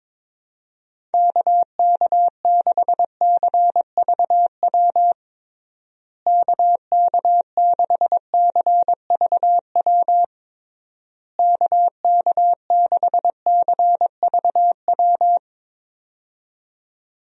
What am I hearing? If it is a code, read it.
Morse code, heard as a keyed tone, about 22 wpm, 700 Hz. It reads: KK6CVW KK6CVW KK6CVW
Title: USA Call Signs - 6 Characters - Repeated 3x - 22wpm